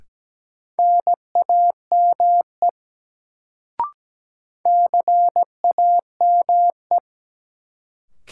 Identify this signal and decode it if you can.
Morse code, heard as a keyed tone, about 17 wpm, 700 Hz. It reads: NAME CAME
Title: Sentences from Top 200 Words - New Word Drill - 17wpm